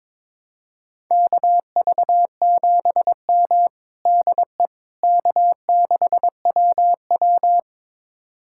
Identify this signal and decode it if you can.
Morse code, heard as a keyed tone, about 21 wpm, 700 Hz. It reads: KV7M DE K6WW
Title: USA Callsign DE Callsign - 4 Characters - 22wpm